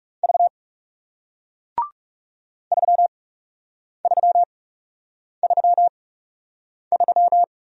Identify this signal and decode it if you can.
Morse code, heard as a keyed tone, about 41 wpm, spaced as for 8 wpm, 700 Hz. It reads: V 3333